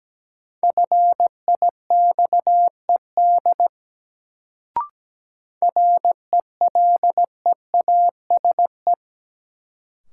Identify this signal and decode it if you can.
Morse code, heard as a keyed tone, about 17 wpm, 700 Hz. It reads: FIXED RELEASE
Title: Sentences from Top 1100 Words - New Word Drill Alt - 17wpm